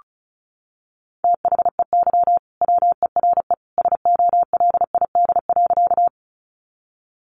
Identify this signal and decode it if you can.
Morse code, heard as a keyed tone, about 34 wpm, 700 Hz. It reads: THEY WERE SOLID.
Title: Sentences from Top 1500 Words 35wpm